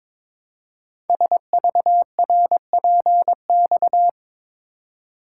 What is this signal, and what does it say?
Morse code, heard as a keyed tone, about 23 wpm, 700 Hz. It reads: SVRPX
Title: Code Groups - 5 Letters - 22wpm